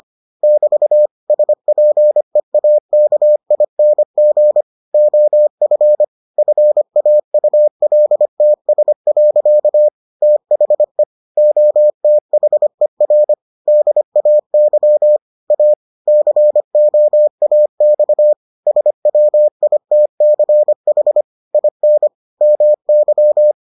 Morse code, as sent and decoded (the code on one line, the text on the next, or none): -...- / ... .--. .- -.- .. -. --. / --- ..-. / ..-. .- ..- .-.. - ... .-.-.- / - .... / --- - .... .-. / -.. .- -.-- / .- / -.-. --- .- -..- / ... .-- .. - -.-. .... / .. -. / -- -.--
<BT> SPEAKING OF FAULTS. THE OTHER DAY A COAX SWITCH IN MY